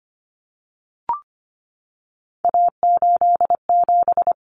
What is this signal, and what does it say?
Morse code, heard as a keyed tone, about 25 wpm, 700 Hz. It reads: A87
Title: Common Cut Numbers - 3-Digit 25wpm